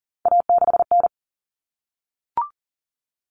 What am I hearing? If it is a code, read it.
Morse code, heard as a keyed tone, about 40 wpm, 700 Hz. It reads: A6N
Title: All Cut Numbers - 3-Digit 40wpm